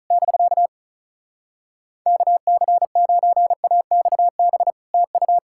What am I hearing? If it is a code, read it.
Morse code, heard as a keyed tone, about 47 wpm, spaced as for 41 wpm, 700 Hz. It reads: <BK> KC9AXB TU
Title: Speed-Racing Common QSO Phrases - 35wpm